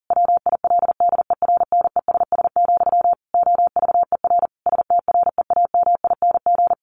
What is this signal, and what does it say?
Morse code, heard as a keyed tone, about 40 wpm, 700 Hz. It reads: WILDERNESS, OVER STREAMING
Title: A Sound Of Thunder 40wpm